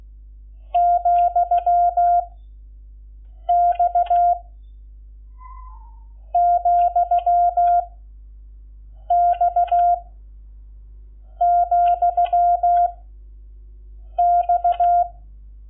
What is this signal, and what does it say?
Morse code, heard as a keyed tone, about 16 wpm, 690 Hz. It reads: , X , X , X
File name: Lecciones / CLASE Nº11